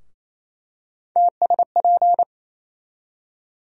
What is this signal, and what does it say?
Morse code, heard as a keyed tone, about 28 wpm, 700 Hz. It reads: TSP
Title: USA Call Sign Postfixes - 3 Letters 28wpm